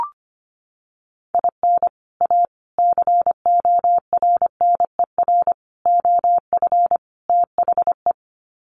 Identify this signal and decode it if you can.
Morse code, heard as a keyed tone, about 25 wpm, 700 Hz. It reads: IN A CORNER OF THE
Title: Sets of 5 Words Encore2 25wpm